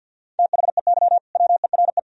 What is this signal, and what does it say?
Morse code, heard as a keyed tone, about 64 wpm, spaced as for 56 wpm, 700 Hz. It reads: THEY WERE
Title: Speed-Racing - Sets of 4 Words 50wpm